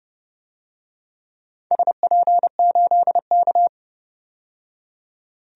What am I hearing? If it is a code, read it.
Morse code, heard as a keyed tone, about 32 wpm, 700 Hz. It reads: SP8K
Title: DXCC-100 Call Signs - 4 Characters - 30wpm